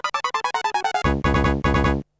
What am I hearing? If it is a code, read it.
There is no code.